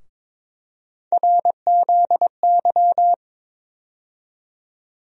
Morse code, heard as a keyed tone, about 22 wpm, 700 Hz. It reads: RZY
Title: USA Call Sign Postfixes - 3 Letters 22wpm